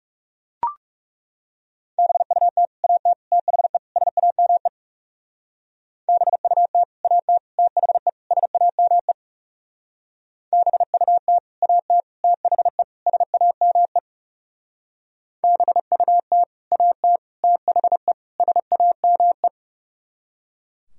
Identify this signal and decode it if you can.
Morse code, heard as a keyed tone, about 37 wpm, 700 Hz. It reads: BUT AT THE SAME BUT AT THE SAME BUT AT THE SAME BUT AT THE SAME
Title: Speed-Racing - Sets of 4 Words 30wpm